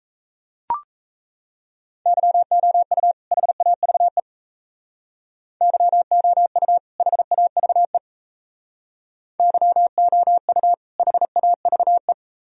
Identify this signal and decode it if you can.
Morse code, heard as a keyed tone, about 36 wpm, 700 Hz. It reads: YOU HAVE YOU HAVE YOU HAVE